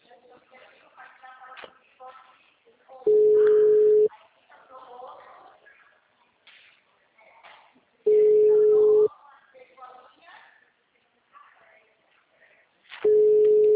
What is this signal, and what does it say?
Ringback tone (Europe, 425 Hz), with speech.
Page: Itu